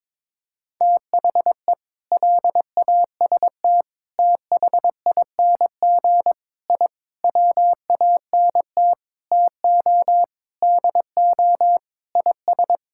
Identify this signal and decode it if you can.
Morse code, heard as a keyed tone, about 21 wpm, 700 Hz. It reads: THE LAST THING I WANT TO DO IS
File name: Sentences with 7-Word Phrases - 22wpm